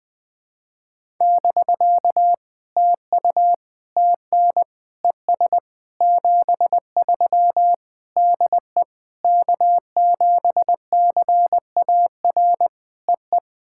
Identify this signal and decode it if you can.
Morse code, heard as a keyed tone, about 20 wpm, 700 Hz. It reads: <BK> TU TN ES 73 DE K7CAR EE
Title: POTA QSO Practice - 20wpm